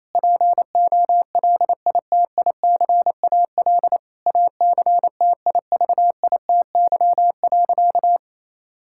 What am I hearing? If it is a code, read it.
Morse code, heard as a keyed tone, about 28 wpm, 700 Hz. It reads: POLITICAL ACTIVITY.
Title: Sentences from Top 1000 Words - Review Part 5 - 28wpm